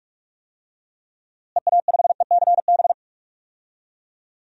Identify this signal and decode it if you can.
Morse code, heard as a keyed tone, about 46 wpm, 700 Hz. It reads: EA5ECB
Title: DXCC-100 Call Signs - 6 Characters - Repeated 3x - 45wpm